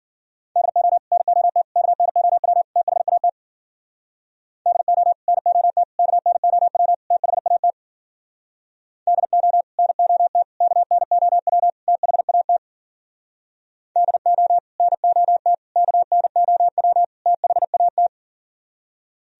Morse code, heard as a keyed tone, about 52 wpm, 700 Hz. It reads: DO NOT KNOW THAT DO NOT KNOW THAT DO NOT KNOW THAT DO NOT KNOW THAT